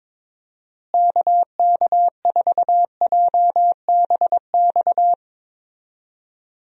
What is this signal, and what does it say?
Morse code, heard as a keyed tone, about 22 wpm, 700 Hz. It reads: KK4JBX
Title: USA Call Signs - 6 Characters - Repeated 2x - 22wpm